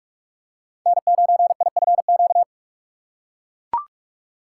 Morse code, heard as a keyed tone, about 41 wpm, 700 Hz. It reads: N9IFQ
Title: USA Call Signs - 5 Characters - 2x Character Spacing - 45wpm